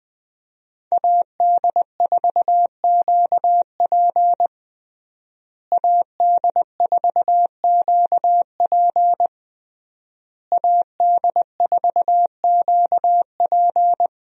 Morse code, heard as a keyed tone, about 20 wpm, 700 Hz. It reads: AD4QP AD4QP AD4QP
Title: USA Call Signs - 5 Characters - Repeated 3x - 20wpm